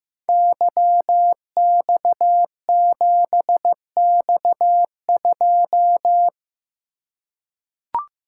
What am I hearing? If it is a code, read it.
Morse code, heard as a keyed tone, about 15 wpm, 700 Hz. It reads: YX7X2
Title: Code Groups - 5 Letter:Numbers - 15wpm